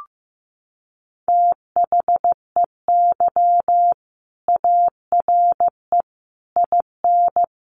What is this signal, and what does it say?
Morse code, heard as a keyed tone, about 14 wpm, 700 Hz. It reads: THEY ARE IN
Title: Sentences from Top 1400 Words 15wpm